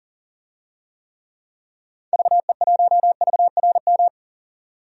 Morse code, heard as a keyed tone, about 40 wpm, 700 Hz. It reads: VE1VRM
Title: DXCC-100 Call Signs - 6 Characters - 40wpm